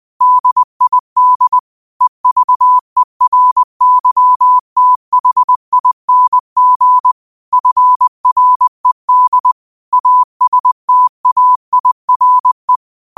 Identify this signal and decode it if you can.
Morse code, heard as a keyed tone, about 20 wpm, 1000 Hz. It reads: DID EVERYTHING FRED ASTAIRE
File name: qotdfiles / 20wpm